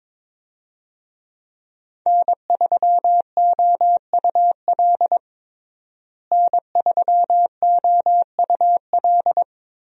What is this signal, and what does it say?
Morse code, heard as a keyed tone, about 23 wpm, 700 Hz. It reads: N3OUL N3OUL
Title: USA Call Signs - 5 Characters - Repeated 3x - 22wpm